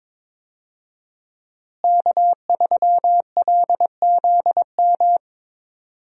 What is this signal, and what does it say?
Morse code, heard as a keyed tone, about 22 wpm, 700 Hz. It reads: K3LZM
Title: USA Call Signs - 5 Characters - Repeated 3x - 22wpm